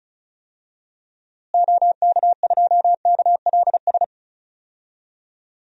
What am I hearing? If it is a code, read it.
Morse code, heard as a keyed tone, about 36 wpm, 700 Hz. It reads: OK2KLS
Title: DXCC-100 Call Signs - 6 Characters - Repeated 3x - 35wpm